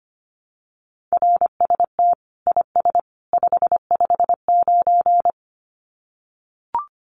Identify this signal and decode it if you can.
Morse code, heard as a keyed tone, about 25 wpm, 700 Hz. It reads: RST IS 559
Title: Common QSO Phrases 25wpm